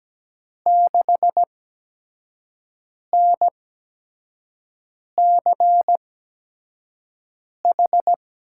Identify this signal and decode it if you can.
Morse code, heard as a keyed tone, about 17 wpm, spaced as for 5 wpm, 700 Hz. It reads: 6NCH
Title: USA Call Signs - 5 Characters - 8x Character Spacing - 17wpm